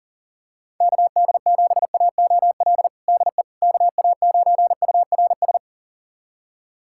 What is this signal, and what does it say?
Morse code, heard as a keyed tone, about 40 wpm, 700 Hz. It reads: KD7AOL DE KA9URS
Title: USA Callsign DE Callsign 40wpm